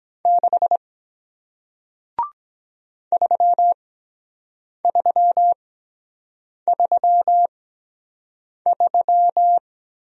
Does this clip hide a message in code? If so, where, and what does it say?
Yes: Morse code from 0.3 to 9.6 s, heard as a keyed tone, about 23 wpm, spaced as for 7 wpm, 700 Hz. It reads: 6 3333